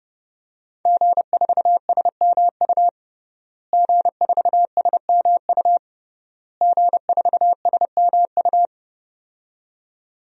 Morse code, heard as a keyed tone, about 30 wpm, 700 Hz. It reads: G4SMU G4SMU G4SMU
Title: DXCC-100 Call Signs - 5 Characters - Repeated 3x - 30wpm